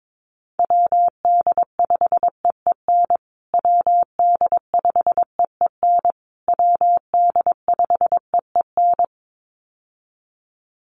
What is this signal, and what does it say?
Morse code, heard as a keyed tone, about 22 wpm, 700 Hz. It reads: WD5EEN WD5EEN WD5EEN
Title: USA Call Signs Repeated 3x 22wpm